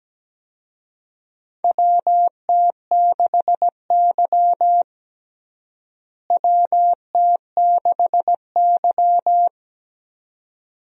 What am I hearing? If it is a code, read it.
Morse code, heard as a keyed tone, about 17 wpm, 700 Hz. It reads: WT6Y WT6Y
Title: USA Call Signs - 4 Characters - Repeated 2x - 17wpm